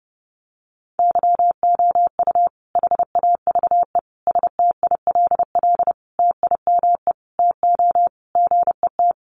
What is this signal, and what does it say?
Morse code, heard as a keyed tone, about 30 wpm, 700 Hz. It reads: YOU HAVE STILL TIME TO GET
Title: Sentences from Top 600 Words 30wpm